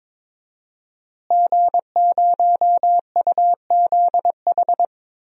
Morse code, heard as a keyed tone, about 22 wpm, 700 Hz. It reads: G0UZH
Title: DXCC-100 Call Signs - 5 Characters - Repeated 2x - 22wpm